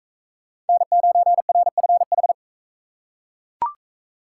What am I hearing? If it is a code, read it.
Morse code, heard as a keyed tone, about 40 wpm, 700 Hz. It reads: N9RFH